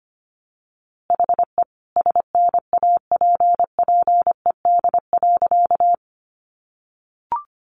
Morse code, heard as a keyed tone, about 25 wpm, 700 Hz. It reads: HE SNAPPED.
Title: A Sound Of Thunder 25wpm